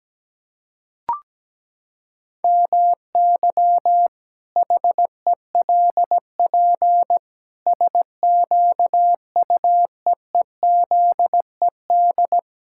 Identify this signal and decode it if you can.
Morse code, heard as a keyed tone, about 17 wpm, 700 Hz. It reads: MY HELP SQUEEZED
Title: Pangram Challenge v2 - 17wpm